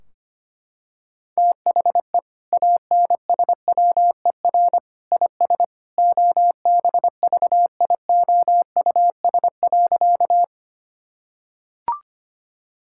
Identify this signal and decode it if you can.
Morse code, heard as a keyed tone, about 25 wpm, 700 Hz. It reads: THE ANSWER IS OBVIOUS.
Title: Sentences from Top 1600 Words 25wpm